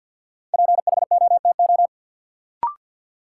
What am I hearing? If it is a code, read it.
Morse code, heard as a keyed tone, about 53 wpm, 700 Hz. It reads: PHOTO
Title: Sentences from Top 1200 Words - New Word Drill - 50wpm